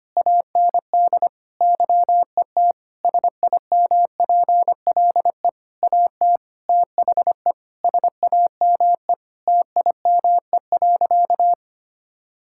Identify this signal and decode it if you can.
Morse code, heard as a keyed tone, about 25 wpm, 700 Hz. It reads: AND YET SIMPLE AT THE SAME TIME.
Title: Sentences from Top 1000 Words - Review Part 6 - 25wpm